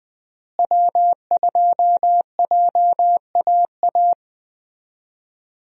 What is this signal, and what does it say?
Morse code, heard as a keyed tone, about 21 wpm, 700 Hz. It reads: W2JAA
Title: USA Call Signs - 5 Characters - Repeated 2x - 20wpm